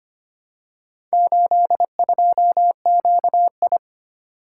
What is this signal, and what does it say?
Morse code, heard as a keyed tone, about 25 wpm, 700 Hz. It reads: 82QI
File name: practice / 25wpm / Four Letter-Number Combinations - 25wpm